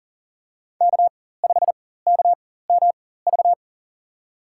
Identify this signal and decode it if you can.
Morse code, heard as a keyed tone, about 37 wpm, spaced as for 18 wpm, 700 Hz. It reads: K5KMV
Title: USA Call Signs - 5 Characters - 4x Character Spacing - 40wpm